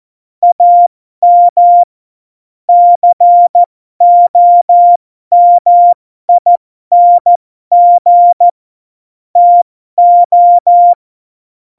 Morse code, heard as a keyed tone, about 14 wpm, spaced as for 12 wpm, 700 Hz. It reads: AM COMING TO
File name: Morsekurs / Witze